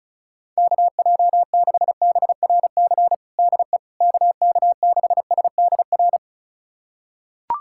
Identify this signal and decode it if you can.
Morse code, heard as a keyed tone, about 36 wpm, 700 Hz. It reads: KJ6BRC DE KK6SDR
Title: USA Callsign DE Callsign - 6 Characters - 35wpm